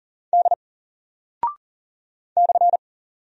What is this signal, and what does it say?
Morse code, heard as a keyed tone, about 40 wpm, 700 Hz. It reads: D /
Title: Single Character (Letter:Number:Punctuation:Prosign) 40wpm